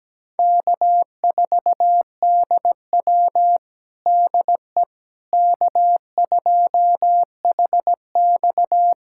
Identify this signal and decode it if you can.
Morse code, heard as a keyed tone, about 17 wpm, 700 Hz. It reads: K4DW DE K2HX
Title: USA Callsign DE Callsign - 4 Characters - 17wpm